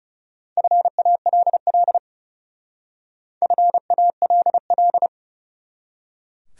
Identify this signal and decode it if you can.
Morse code, heard as a keyed tone, about 40 wpm, 700 Hz. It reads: FALL FALL